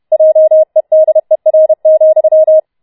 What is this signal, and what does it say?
Morse code, heard as a keyed tone, about 31 wpm, 630 Hz. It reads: JEDER,